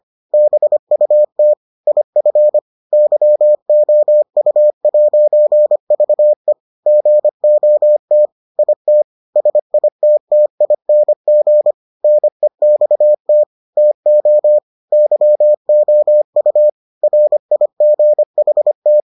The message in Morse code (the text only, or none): BUT IF YOU'VE GOT IT SITTING NEXT TO YOU RIGHT